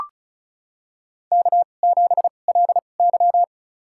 Morse code, heard as a keyed tone, about 34 wpm, spaced as for 25 wpm, 700 Hz. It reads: K7LY
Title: USA Call Signs - 4 Characters - 2x Character Spacing - 35wpm